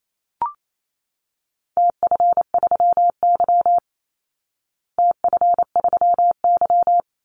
Morse code, heard as a keyed tone, about 28 wpm, 700 Hz. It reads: TF3Y TF3Y
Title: Call Signs Repeated 3x - 28wpm